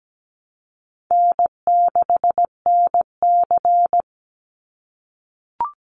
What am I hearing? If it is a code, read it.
Morse code, heard as a keyed tone, about 17 wpm, 700 Hz. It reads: N6NC
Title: Easy Call Signs - 17wpm